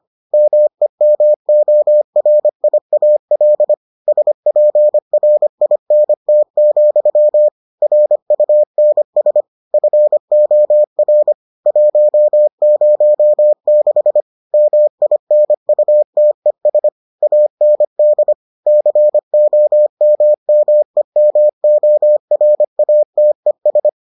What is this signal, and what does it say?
Morse code, heard as a keyed tone, about 25 wpm, 600 Hz. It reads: MEMORIAL SPRINT, RUNS FOR 106 MINUTES AND COMMEMORATES